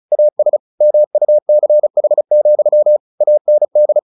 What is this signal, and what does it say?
Morse code, heard as a keyed tone, about 34 wpm, 600 Hz. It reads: AS MUCH, AND
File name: the old man and the sea / the old man and the sea 1 35wpm